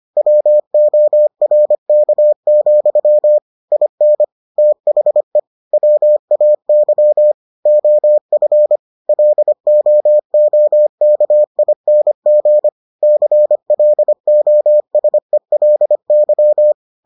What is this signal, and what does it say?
Morse code, heard as a keyed tone, about 25 wpm, 600 Hz. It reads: WORK, IN THE WAY OF LOOKING CLOSELY